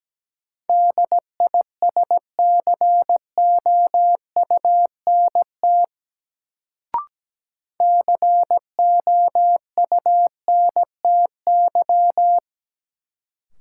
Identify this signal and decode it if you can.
Morse code, heard as a keyed tone, about 17 wpm, 700 Hz. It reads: DISCOUNT COUNTY
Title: Sentences from Top 1200 Words - New Word Drill - 17wpm